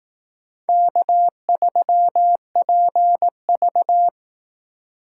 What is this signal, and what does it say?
Morse code, heard as a keyed tone, about 18 wpm, 700 Hz. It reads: K3PV
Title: Speed Racing - USA 1x2 Call Signs 15wpm